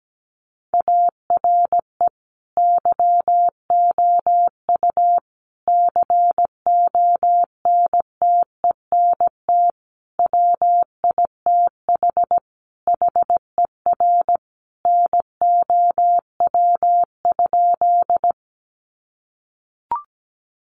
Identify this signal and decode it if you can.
Morse code, heard as a keyed tone, about 17 wpm, 700 Hz. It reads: ARE YOU CONTENT WITH HER NOW?
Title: Sentences from Top 900 Words 17wpm